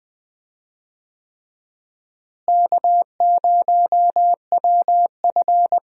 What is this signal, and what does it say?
Morse code, heard as a keyed tone, about 21 wpm, 700 Hz. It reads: K0WF
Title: USA Call Signs - 4 Characters - Repeated 3x - 20wpm